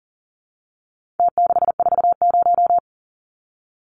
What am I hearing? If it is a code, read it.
Morse code, heard as a keyed tone, about 40 wpm, 700 Hz. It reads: T640
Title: Numbers - Contest Style - 4-Digit 40wpm